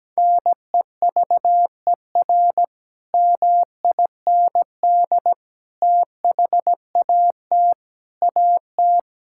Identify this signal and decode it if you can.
Morse code, heard as a keyed tone, about 17 wpm, 700 Hz. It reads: NEVER MIND THAT AT
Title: Sentences from Top 1000 Words - Review Part 5 - 17wpm